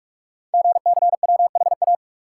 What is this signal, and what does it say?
Morse code, heard as a keyed tone, about 44 wpm, 700 Hz. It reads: GCWHA
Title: Code Groups - 5 Letter:Numbers - 45wpm